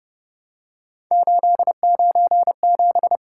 Morse code, heard as a keyed tone, about 30 wpm, 700 Hz. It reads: 897
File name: Speed-Racing - Three Letter-Number Combinations - 25wpm